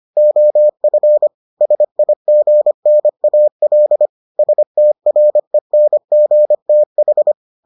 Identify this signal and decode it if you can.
Morse code, heard as a keyed tone, about 25 wpm, 600 Hz. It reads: OF SIGNAL STRENGTH